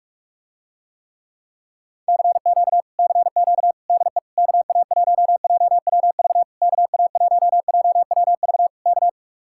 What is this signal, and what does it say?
Morse code, heard as a keyed tone, about 49 wpm, 700 Hz. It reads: CQ CQ DE KA1JWV KA1JWV K